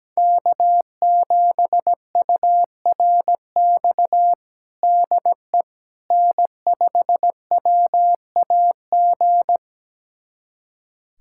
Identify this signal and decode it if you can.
Morse code, heard as a keyed tone, about 17 wpm, 700 Hz. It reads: K7URX DE N5WAG
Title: USA Callsign DE Callsign - 5 Characters - 17wpm